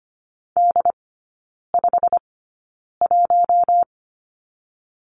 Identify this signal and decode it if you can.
Morse code, heard as a keyed tone, about 25 wpm, spaced as for 9 wpm, 690 Hz. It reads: D51
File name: Lesson 15e - Number 5 - Three Characters 25wpm